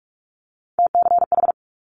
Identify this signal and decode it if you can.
Morse code, heard as a keyed tone, about 46 wpm, 700 Hz. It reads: TCH
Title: Multi-letter Phonemes Quick Fire 45wpm